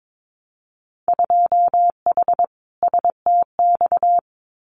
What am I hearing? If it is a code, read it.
Morse code, heard as a keyed tone, about 22 wpm, 700 Hz. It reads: 2H STX
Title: Winter Field Day Exchanges Encore 22wpm